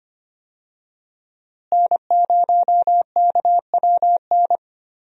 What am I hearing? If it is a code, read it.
Morse code, heard as a keyed tone, about 25 wpm, 700 Hz. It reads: N0KWN
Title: USA Call Signs - 5 Characters - Repeated 2x - 25wpm